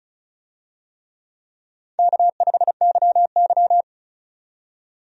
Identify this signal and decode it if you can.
Morse code, heard as a keyed tone, about 36 wpm, 700 Hz. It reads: K5YY
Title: USA Call Signs - 4 Characters - 35wpm